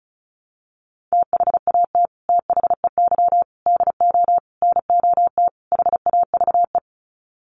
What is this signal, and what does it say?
Morse code, heard as a keyed tone, about 34 wpm, 700 Hz. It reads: THAT THEY DO NOT HAVE
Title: Sets of 5 Words Encore2 35wpm